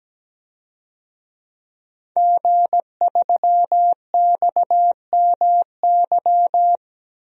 Morse code, heard as a keyed tone, about 17 wpm, 700 Hz. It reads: G3XMY
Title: DXCC-100 Call Signs - 5 Characters - Repeated 2x - 17wpm